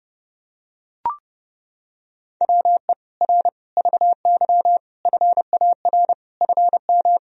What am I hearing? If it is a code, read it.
Morse code, heard as a keyed tone, about 30 wpm, 700 Hz. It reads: WE R VY FAR FM
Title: Sentences from Top 300 Words and Abbreviations 30wpm